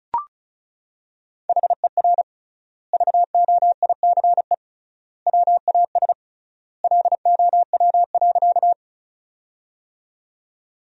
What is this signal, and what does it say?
Morse code, heard as a keyed tone, about 36 wpm, 700 Hz. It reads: HER VOICE WAS LOW.
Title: 3x Word-Spacing - Sentences from Top 500 Words - 35wpm